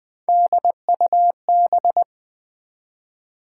Morse code, heard as a keyed tone, about 20 wpm, 700 Hz. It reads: DUB
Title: Build A Word - 1 Letter At A Time - 3-Letter Words - 20wpm